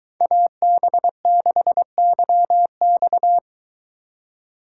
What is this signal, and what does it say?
Morse code, heard as a keyed tone, about 22 wpm, 700 Hz. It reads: AB6YX